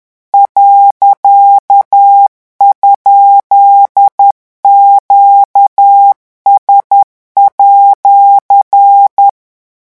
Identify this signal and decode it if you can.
Morse code, heard as a keyed tone, about 10 wpm, 800 Hz. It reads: .?QS@
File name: morse mp3 / revision19-vitesse 10 mots